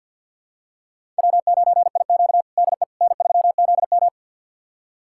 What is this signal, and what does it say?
Morse code, heard as a keyed tone, about 50 wpm, 700 Hz. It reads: W9IQ DE N3ZM
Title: USA Callsign DE Callsign - 4 Characters - 50wpm